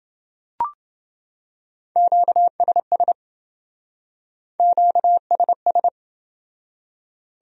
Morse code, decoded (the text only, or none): QSS QSS